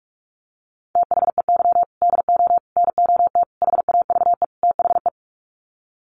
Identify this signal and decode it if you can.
Morse code, heard as a keyed tone, about 44 wpm, 700 Hz. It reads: THEY DO NOT HAVE THE
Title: Sets of 5 Words 45wpm